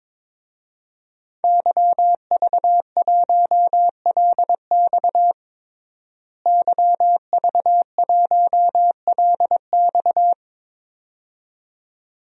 Morse code, heard as a keyed tone, about 22 wpm, 700 Hz. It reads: YV1LX YV1LX
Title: DXCC-100 Call Signs - 5 Characters - Repeated 2x - 22wpm